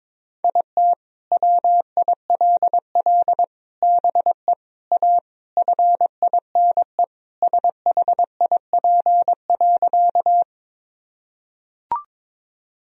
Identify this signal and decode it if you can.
Morse code, heard as a keyed tone, about 22 wpm, 700 Hz. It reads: IT WILL BE A FINE SHIP.
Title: Sentences from Top 1000 Words - Review Part 2 - 22wpm